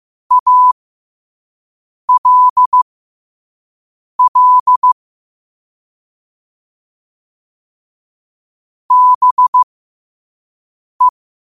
Morse code, heard as a keyed tone, about 15 wpm, spaced as for 5 wpm, 1000 Hz. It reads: ALL BE
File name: qotdfiles / 5wpm